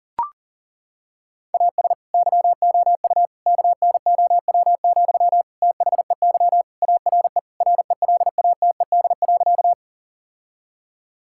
Morse code, heard as a keyed tone, about 40 wpm, 700 Hz. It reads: AS YOU KNOW, THEY ARE RELATED.
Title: Sentences from Top 1000 Words - Review Part 5 - 40wpm